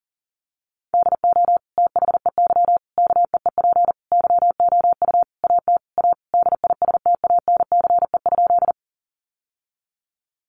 Morse code, heard as a keyed tone, about 40 wpm, 700 Hz. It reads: DO THEY KEEP YOU AT A DISTANCE?
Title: Sentences from Top 1200 Words 40wpm